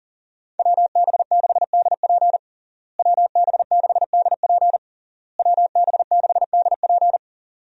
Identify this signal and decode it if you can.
Morse code, heard as a keyed tone, about 40 wpm, 700 Hz. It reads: WB6DP WB6DP WB6DP
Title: USA Call Signs - 5 Characters - Repeated 3x - 40wpm